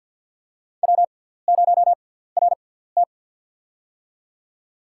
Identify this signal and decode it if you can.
Morse code, heard as a keyed tone, about 52 wpm, spaced as for 17 wpm, 700 Hz. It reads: W0RT
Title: USA Call Signs - 6x Character Spacing - 50wpm